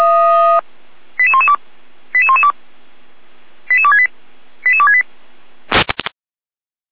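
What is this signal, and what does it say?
Five-tone selective call (ZVEI), heard as quick runs of five tones, tones of about 70 ms. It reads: 88112 88112 88268 88268
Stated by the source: ZVEI five-tone selective call received with an SDR, from the audio server of Feuerwehr Orlamünde.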